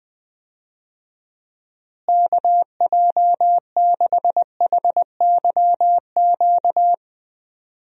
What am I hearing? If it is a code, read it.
Morse code, heard as a keyed tone, about 20 wpm, 700 Hz. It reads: KJ6HYQ
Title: USA Call Signs - 6 Characters - Repeated 2x - 20wpm